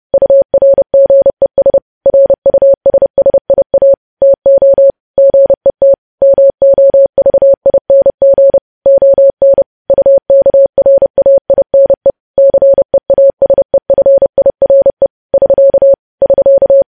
Morse code, heard as a keyed tone, about 30 wpm, 550 Hz. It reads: URGES RUSSIA TO GET MOVING ON UKRAINE CEASEFIRE <SK> <SK>
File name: podcast / News headlines-1x-30x30